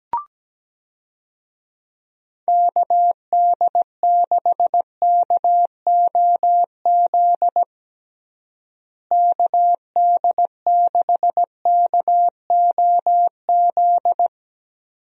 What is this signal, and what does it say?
Morse code, heard as a keyed tone, about 17 wpm, 700 Hz. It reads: KD6KOZ KD6KOZ
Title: USA Call Signs - 6 Characters - Repeated 2x - 17wpm